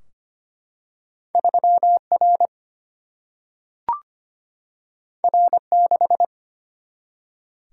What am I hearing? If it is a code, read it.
Morse code, heard as a keyed tone, about 25 wpm, 700 Hz. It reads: 3 R R 6
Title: Single-Letter Single-Number 25wpm